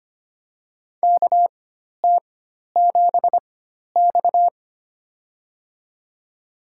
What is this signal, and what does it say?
Morse code, heard as a keyed tone, about 25 wpm, spaced as for 12 wpm, 700 Hz. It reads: KT7X
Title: USA Call Signs - 4 Characters - 4x Character Spacing - 25wpm